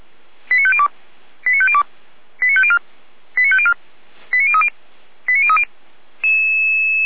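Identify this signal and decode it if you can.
Five-tone selective call (ZVEI), heard as quick runs of five tones, tones of about 80 ms. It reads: 89502 89502 89504 89504 89030 89030
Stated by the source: ZVEI five-tone selective call 89033 received by SDR, fire brigade feuerwehr-orlamuende.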